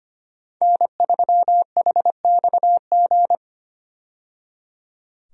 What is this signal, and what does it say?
Morse code, heard as a keyed tone, about 25 wpm, 700 Hz. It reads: N3HXG
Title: USA Call Signs - 5 Characters - 25wpm